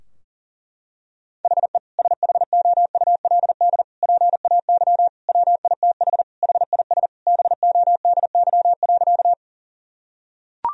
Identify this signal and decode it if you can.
Morse code, heard as a keyed tone, about 37 wpm, 700 Hz. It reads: HE SHOULD PAY WITH HIS BODY.
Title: Sentences from Top 1000 Words - Review Part 2 - 40wpm